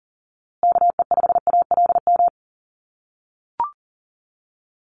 Morse code, heard as a keyed tone, about 38 wpm, 700 Hz. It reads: KE5ALM
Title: USA Call Signs Repeated 3x 40wpm